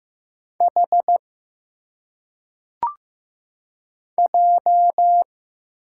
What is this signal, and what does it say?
Morse code, heard as a keyed tone, about 15 wpm, 700 Hz. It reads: H J
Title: Single Letters - Phonetic Pronunciation 15wpm